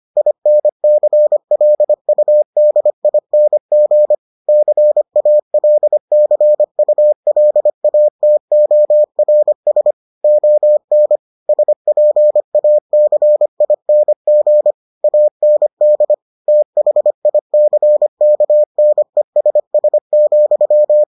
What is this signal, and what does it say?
Morse code, heard as a keyed tone, about 25 wpm, 600 Hz. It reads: INCLUDING CALCULATORS ON SPACING AND THICKNESS,